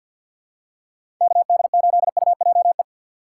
Morse code, heard as a keyed tone, about 53 wpm, 700 Hz. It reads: KD8UJE
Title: USA Call Signs - 6 Characters - 4x Character Spacing - 50wpm